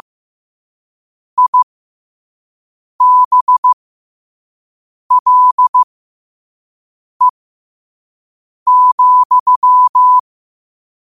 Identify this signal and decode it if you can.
Morse code, heard as a keyed tone, about 15 wpm, spaced as for 5 wpm, 1000 Hz. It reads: IBLE,
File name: qotdfiles / 5wpm